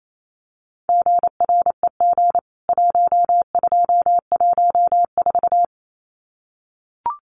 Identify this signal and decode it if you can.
Morse code, heard as a keyed tone, about 28 wpm, 700 Hz. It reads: GREG 1214
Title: CWT Exchange 28wpm